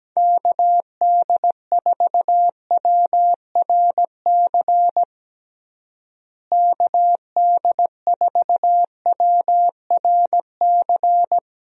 Morse code, heard as a keyed tone, about 17 wpm, 700 Hz. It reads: KD4WRC KD4WRC
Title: USA Call Signs - 6 Characters - Repeated 3x - 17wpm